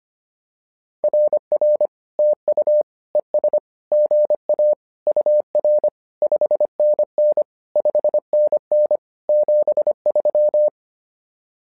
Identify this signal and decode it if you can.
Morse code, heard as a keyed tone, about 25 wpm, 600 Hz. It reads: RR TU ES GA UR 5NN 5NN 73
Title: SOTA QSO Practice 25wpm